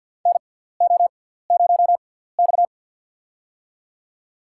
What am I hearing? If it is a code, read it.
Morse code, heard as a keyed tone, about 53 wpm, spaced as for 17 wpm, 700 Hz. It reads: NO0X
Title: USA Call Signs - 6x Character Spacing - 50wpm